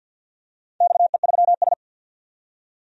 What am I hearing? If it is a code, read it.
Morse code, heard as a keyed tone, about 51 wpm, 700 Hz. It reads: XE3S
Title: DXCC-100 Call Signs - 4 Characters - 50wpm